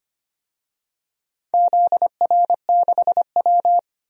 Morse code, heard as a keyed tone, about 25 wpm, 700 Hz. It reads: ZR6W
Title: DXCC-100 Call Signs - 4 Characters - 25wpm